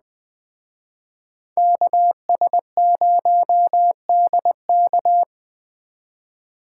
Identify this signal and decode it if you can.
Morse code, heard as a keyed tone, about 20 wpm, 700 Hz. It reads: KS0DK
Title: Speed Racing - USA 2x2 Call Signs 17wpm